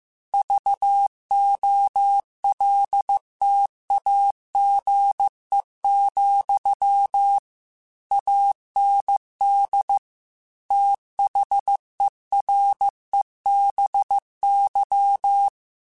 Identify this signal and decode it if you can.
Morse code, heard as a keyed tone, about 15 wpm, 800 Hz. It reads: VOLTAGE, AND THEREBY
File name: Technical Subjects / Technical Subjects-mp3s / XL @13